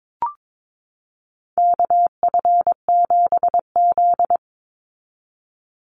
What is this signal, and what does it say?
Morse code, heard as a keyed tone, about 22 wpm, 700 Hz. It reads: KF7Z
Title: Call Signs Repeated 3x - 22wpm